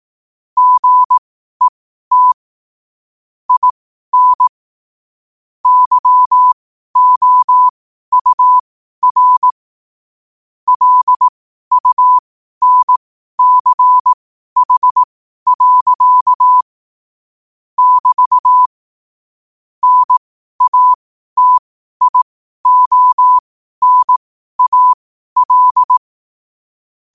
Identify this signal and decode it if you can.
Morse code, heard as a keyed tone, about 18 wpm, spaced as for 12 wpm, 1000 Hz. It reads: GET IN YOUR LUNCH. <BT> NATIONAL
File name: qotdfiles / 12wpm